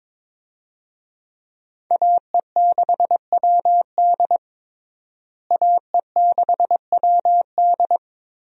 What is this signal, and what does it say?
Morse code, heard as a keyed tone, about 22 wpm, 700 Hz. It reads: AE6WD AE6WD
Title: USA Call Signs - 5 Characters - Repeated 3x - 22wpm